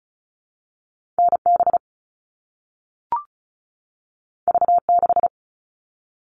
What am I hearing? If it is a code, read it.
Morse code, heard as a keyed tone, about 36 wpm, 700 Hz. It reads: NB V6
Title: All Cut Numbers - 2-Digit 35wpm